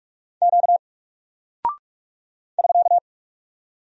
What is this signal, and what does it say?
Morse code, heard as a keyed tone, about 44 wpm, 700 Hz. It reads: Q <SK>
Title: Single Character (Letter:Number:Punctuation:Prosign) 45wpm